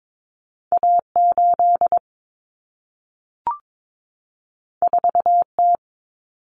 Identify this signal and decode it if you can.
Morse code, heard as a keyed tone, about 22 wpm, 700 Hz. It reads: A8 4T
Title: Common Cut Numbers - 2-Digit 22wpm